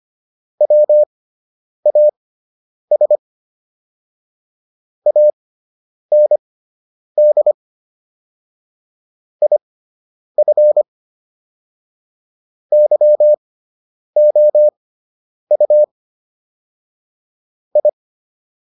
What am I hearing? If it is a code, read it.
Morse code, heard as a keyed tone, about 25 wpm, spaced as for 9 wpm, 600 Hz. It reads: WAS AND IF YOU I